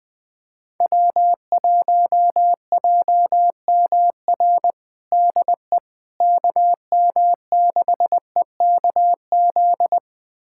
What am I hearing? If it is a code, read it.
Morse code, heard as a keyed tone, about 20 wpm, 700 Hz. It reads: W1JMR DE KM6EKZ
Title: USA Callsign DE Callsign 20wpm